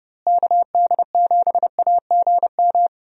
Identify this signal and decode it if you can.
Morse code, heard as a keyed tone, about 30 wpm, 700 Hz. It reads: KD7AGM
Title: USA Call Signs - 6 Characters - Repeated 2x - 30wpm